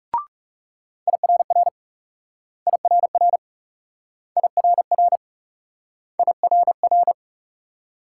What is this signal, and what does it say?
Morse code, heard as a keyed tone, about 38 wpm, 700 Hz. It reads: IRR IRR IRR IRR